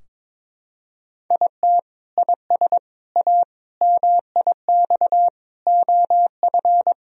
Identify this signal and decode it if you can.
Morse code, heard as a keyed tone, about 22 wpm, 700 Hz. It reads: IT IS A MIX OF
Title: Sentences from Top 1000 Words - Review Part 4 - 22wpm